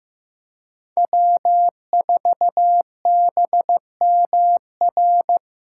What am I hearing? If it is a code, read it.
Morse code, heard as a keyed tone, about 15 wpm, 700 Hz. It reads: W4BMR
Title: USA Callsign DE Callsign - 5 Characters - 15wpm